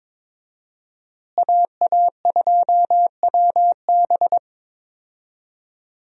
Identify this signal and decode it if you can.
Morse code, heard as a keyed tone, about 21 wpm, 700 Hz. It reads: AA2WB
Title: USA Call Signs - 5 Characters - Repeated 3x - 22wpm